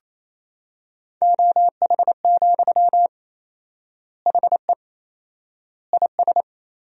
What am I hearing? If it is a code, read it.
Morse code, heard as a keyed tone, about 28 wpm, 700 Hz. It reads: OH, HE IS
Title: 4x Word-Spacing - Sentences from Top 500 Words - 28wpm